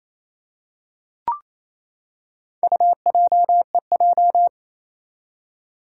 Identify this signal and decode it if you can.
Morse code, heard as a keyed tone, about 28 wpm, 700 Hz. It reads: UJEJ
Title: Four Challenging Letter-Number Combinations 28wpm